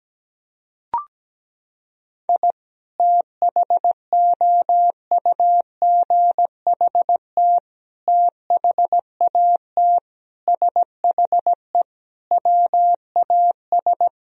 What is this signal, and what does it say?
Morse code, heard as a keyed tone, about 17 wpm, 700 Hz. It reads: I THOUGHT THAT SHE WAS
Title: Sentences from Top 1000 Words - Review Part 3 - 17wpm